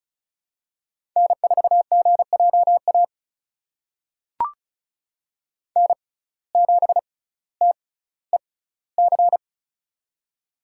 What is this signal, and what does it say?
Morse code, heard as a keyed tone, about 34 wpm, spaced as for 18 wpm, 700 Hz. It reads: N4GJA N7TEC
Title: USA Call Signs - 5 Characters - 6x Character Spacing - 35wpm